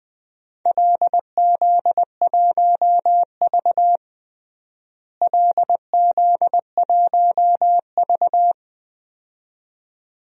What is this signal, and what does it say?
Morse code, heard as a keyed tone, about 20 wpm, 700 Hz. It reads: LZ1V LZ1V
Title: DXCC-100 Call Signs - 4 Characters - Repeated 2x - 20wpm